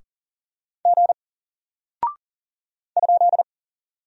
Morse code, heard as a keyed tone, about 39 wpm, 700 Hz. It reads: G ?
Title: Single Character (Letter:Number:Punctuation:Prosign) Alt 40wpm